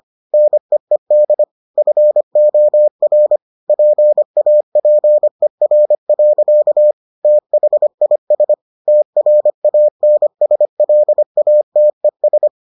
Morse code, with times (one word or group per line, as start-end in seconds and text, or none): 0.3-1.4 NEED
1.8-3.4 FOR
3.7-6.9 PAPER.
7.2-8.5 THIS
8.9-12.5 TRANSLATES